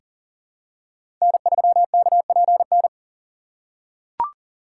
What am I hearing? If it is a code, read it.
Morse code, heard as a keyed tone, about 40 wpm, 700 Hz. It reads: N3KPN
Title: USA Call Signs - 5 Characters - 4x Character Spacing - 40wpm